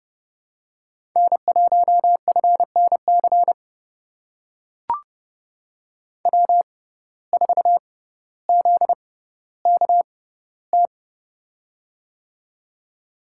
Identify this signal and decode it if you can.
Morse code, heard as a keyed tone, about 30 wpm, spaced as for 15 wpm, 700 Hz. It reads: N1FNC W4ZKT
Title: USA Call Signs - 5 Characters - 6x Character Spacing - 30wpm